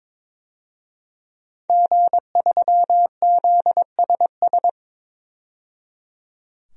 Morse code, heard as a keyed tone, about 22 wpm, 700 Hz. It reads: G3ZSS
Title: DXCC-100 Call Signs - 5 Characters - 22wpm